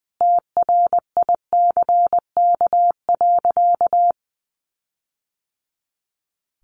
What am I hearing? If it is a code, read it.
Morse code, heard as a keyed tone, about 20 wpm, 700 Hz. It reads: TRICK.
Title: Sentences from Top 1500 Words 20wpm